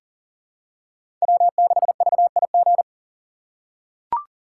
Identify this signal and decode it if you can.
Morse code, heard as a keyed tone, about 40 wpm, 700 Hz. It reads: W6VIG
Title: USA Call Signs - 5 Characters - 2x Character Spacing - 40wpm